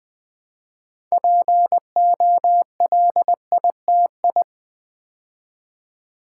Morse code, heard as a keyed tone, about 20 wpm, 700 Hz. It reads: POLITI
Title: Build A Word - 3 Letters At A Time - 9-Letter Words - 20wpm